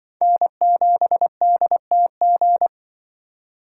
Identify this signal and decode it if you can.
Morse code, heard as a keyed tone, about 24 wpm, 700 Hz. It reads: N7DTG
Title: Speed Racing - USA Call Signs 20wpm